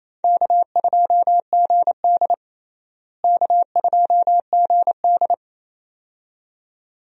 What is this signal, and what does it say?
Morse code, heard as a keyed tone, about 28 wpm, 700 Hz. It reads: K2GD K2GD
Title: USA Call Signs - 4 Characters - Repeated 2x - 28wpm